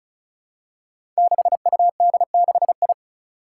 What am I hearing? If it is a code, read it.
Morse code, heard as a keyed tone, about 36 wpm, 700 Hz. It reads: 6UD6I
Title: Five Challenging Letter-Number Combinations 35wpm